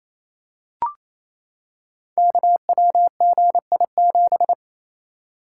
Morse code, heard as a keyed tone, about 27 wpm, 700 Hz. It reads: KWGI7
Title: Code Groups - 5 Letter:Numbers - 28wpm